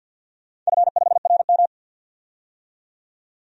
Morse code, heard as a keyed tone, about 50 wpm, 700 Hz. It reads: F5RM
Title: DXCC-100 Call Signs - 4 Characters - 50wpm